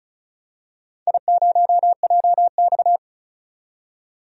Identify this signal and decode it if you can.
Morse code, heard as a keyed tone, about 36 wpm, 700 Hz. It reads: I0JX
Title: DXCC-100 Call Signs - 4 Characters - 35wpm